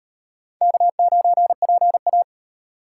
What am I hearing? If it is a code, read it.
Morse code, heard as a keyed tone, about 36 wpm, 700 Hz. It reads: K9PA